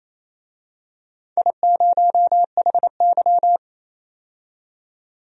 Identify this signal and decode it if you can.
Morse code, heard as a keyed tone, about 28 wpm, 700 Hz. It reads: I0HY
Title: DXCC-100 Call Signs - 4 Characters - 28wpm